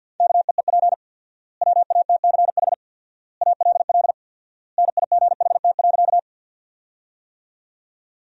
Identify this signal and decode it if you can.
Morse code, heard as a keyed tone, about 53 wpm, 700 Hz. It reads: KEEP WATCH ALL NIGHT.
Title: 4x Word-Spacing - Sentences from Top 300 Words - 50wpm